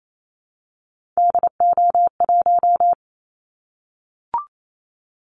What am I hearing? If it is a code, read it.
Morse code, heard as a keyed tone, about 28 wpm, 700 Hz. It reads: DO1
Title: Callsign Prefixes for Top 25 DXCC - Repeated 3x - 28wpm